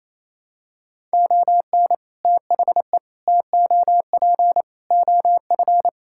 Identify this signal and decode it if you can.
Morse code, heard as a keyed tone, about 28 wpm, 700 Hz. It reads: ON THE TOP OF
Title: Sentences from Top 1000 Words - Review Part 6 - 28wpm